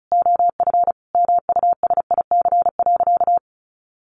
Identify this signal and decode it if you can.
Morse code, heard as a keyed tone, about 34 wpm, 700 Hz. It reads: OF MUSIC.
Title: Sentences from Top 1200 Words 35wpm